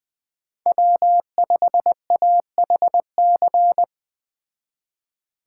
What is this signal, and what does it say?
Morse code, heard as a keyed tone, about 20 wpm, 700 Hz. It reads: W5AHC
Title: Speed Racing - USA 1x3 Call Signs 17wpm